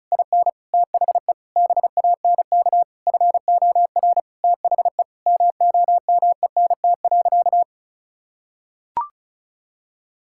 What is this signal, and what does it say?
Morse code, heard as a keyed tone, about 36 wpm, 700 Hz. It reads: IN THE BANK FOR THE MOMENT.
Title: Sentences from Top 1000 Words - Review Part 6 - 35wpm